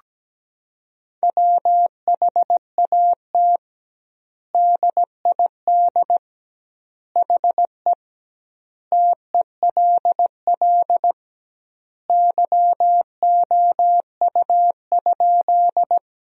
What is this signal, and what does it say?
Morse code, heard as a keyed tone, about 17 wpm, 700 Hz. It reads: WHAT DID HE TELL YOU?
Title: 2x Word-Spacing - Sentences from Top 200 Words - 17wpm